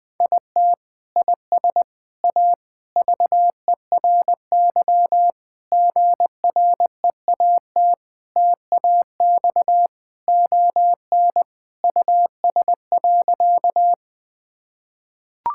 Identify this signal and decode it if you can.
Morse code, heard as a keyed tone, about 21 wpm, 700 Hz. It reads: IT IS A VERY GREAT TAX ON US.
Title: Sentences from Top 1000 Words - Review Part 5 - 20wpm